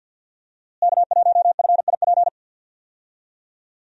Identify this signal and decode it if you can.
Morse code, heard as a keyed tone, about 46 wpm, 700 Hz. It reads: K1FIP
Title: USA Call Signs - 5 Characters - Repeated 3x - 50wpm